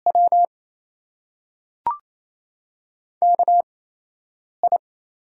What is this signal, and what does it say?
Morse code, heard as a keyed tone, about 28 wpm, spaced as for 8 wpm, 700 Hz. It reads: W KI
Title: USA Call Signs - 6 Characters - 8x Character Spacing - 28wpm